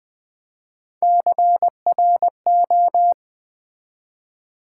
Morse code, heard as a keyed tone, about 20 wpm, 700 Hz. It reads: CRO
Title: Build A Word - 1 Letter At A Time - 5-Letter Words - 20wpm